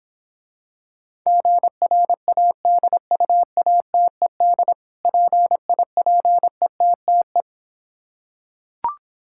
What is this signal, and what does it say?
Morse code, heard as a keyed tone, about 26 wpm, 700 Hz. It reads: GRADUATED PIPETTE